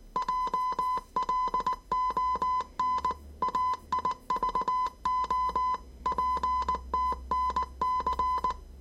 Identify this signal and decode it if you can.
Morse code, heard as a keyed tone, about 19 wpm, 1010 Hz. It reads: JLON AIVO PTNC
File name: 业余无线电 / ●未分类 / 字码80